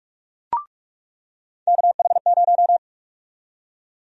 Morse code, heard as a keyed tone, about 44 wpm, 700 Hz. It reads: KH0
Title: USA Call Sign Prefixes - 45wpm